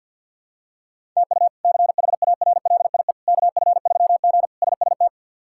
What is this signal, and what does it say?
Morse code, heard as a keyed tone, about 49 wpm, 700 Hz. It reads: TU CHARLIE KF3G SST
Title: K1USN SST QSO Practice - 50wpm